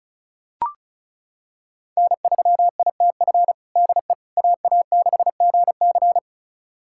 Morse code, heard as a keyed tone, about 36 wpm, 690 Hz. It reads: N3ITF DE AA6GC
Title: USA Callsign DE Callsign - 5 Characters - 35wpm